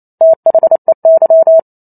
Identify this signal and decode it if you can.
Morse code, heard as a keyed tone, about 28 wpm, 650 Hz. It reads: THEY